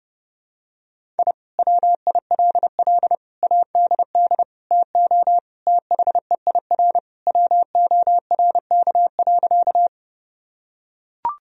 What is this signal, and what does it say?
Morse code, heard as a keyed tone, about 30 wpm, 700 Hz. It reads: I WILL ADD TO THEIR WORK.
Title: Sentences from Top 1000 Words - Review Part 1 - 30wpm